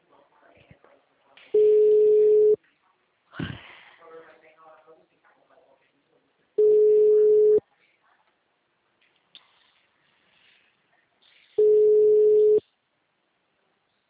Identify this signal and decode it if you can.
Ringback tone (Europe, 425 Hz), with speech.
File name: out- -2007- - - .26